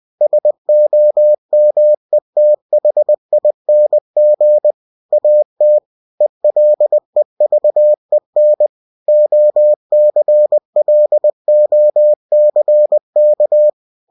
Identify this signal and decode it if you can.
Morse code, heard as a keyed tone, about 20 wpm, 600 Hz. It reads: SOMETHING AT ELEVEN OCLOCK
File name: cw-books / winnie-the-pooh / winnie-20wpm-page-0015